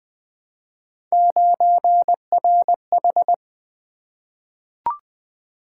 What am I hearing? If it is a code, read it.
Morse code, heard as a keyed tone, about 20 wpm, 700 Hz. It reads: 9RH
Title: Three Letter-Number Combinations - 20wpm